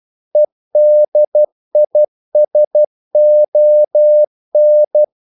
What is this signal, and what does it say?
Morse code, heard as a keyed tone, about 12 wpm, 600 Hz. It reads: EDISON
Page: English quotations in CW (telegraphy - morse)